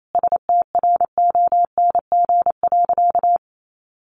Morse code, heard as a keyed tone, about 28 wpm, 700 Hz. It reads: STRONG.
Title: Sentences from Top 1400 Words 28wpm